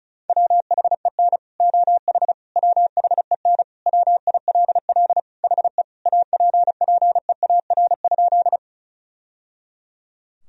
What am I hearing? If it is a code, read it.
Morse code, heard as a keyed tone, about 36 wpm, 700 Hz. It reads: WHEN OH WHEN WILL HE APPEAR?
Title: Sentences from Top 1000 Words - Review Part 2 - 35wpm